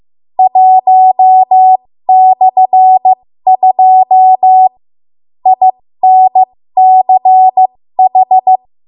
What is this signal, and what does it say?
Morse code, heard as a keyed tone, about 15 wpm, spaced as for 13 wpm, 750 Hz. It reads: 1/2 INCH
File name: Morse / 13WPM